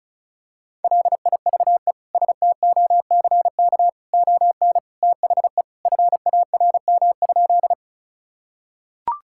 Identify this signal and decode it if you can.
Morse code, heard as a keyed tone, about 36 wpm, 700 Hz. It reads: LIVESTOCK ON THE FARM?
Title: Sentences from Top 1000 Words - Review Part 5 - 35wpm